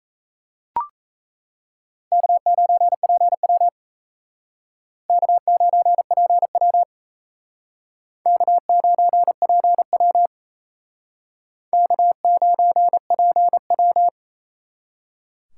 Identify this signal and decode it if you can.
Morse code, heard as a keyed tone, about 36 wpm, 700 Hz. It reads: K9PW K9PW K9PW K9PW